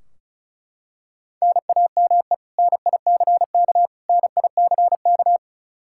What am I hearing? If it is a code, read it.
Morse code, heard as a keyed tone, about 36 wpm, 700 Hz. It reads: NAME NICK NICK
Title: Practice QSO - Exchange 1 - Beginning - 35wpm